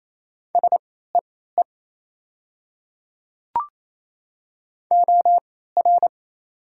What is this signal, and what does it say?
Morse code, heard as a keyed tone, about 28 wpm, spaced as for 16 wpm, 690 Hz. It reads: SEE OR
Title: 3x Character Spacing - Top 100 Words - 28wpm